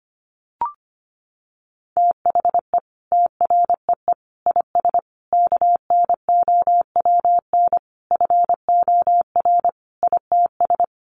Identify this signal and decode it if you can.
Morse code, heard as a keyed tone, about 25 wpm, 700 Hz. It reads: THE TREE IS KNOWN FOR ITS
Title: Sentences from Top 1300 Words 25wpm